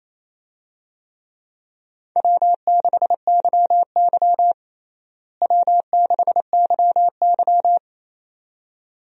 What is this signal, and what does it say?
Morse code, heard as a keyed tone, about 28 wpm, 700 Hz. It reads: W6YY W6YY
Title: USA Call Signs - 4 Characters - Repeated 2x - 28wpm